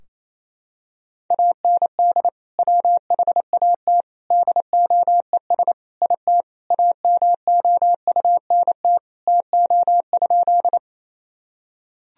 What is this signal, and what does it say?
Morse code, heard as a keyed tone, about 28 wpm, 700 Hz. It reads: AND WHAT DOES IT AMOUNT TO?
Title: Sentences from Top 1000 Words - Review Part 4 - 28wpm